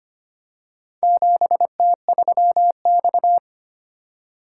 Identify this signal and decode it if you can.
Morse code, heard as a keyed tone, about 25 wpm, 700 Hz. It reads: 7T3X
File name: practice / 25wpm / Four Letter-Number Combinations - 25wpm